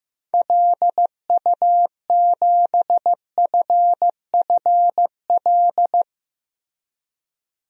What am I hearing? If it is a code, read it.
Morse code, heard as a keyed tone, about 15 wpm, 700 Hz. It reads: LU7FFL
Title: DXCC-100 Call Signs - 6 Characters - Repeated 3x - 15wpm